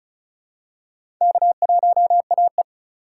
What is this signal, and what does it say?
Morse code, heard as a keyed tone, about 35 wpm, 700 Hz. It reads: K1AE
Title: USA Call Signs - 4 Characters - 4x Character Spacing - 35wpm